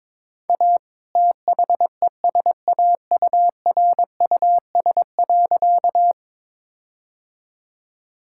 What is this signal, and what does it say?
Morse code, heard as a keyed tone, about 22 wpm, 700 Hz. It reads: A THESAURUS.
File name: All-Age Appropriate Jokes - 22wpm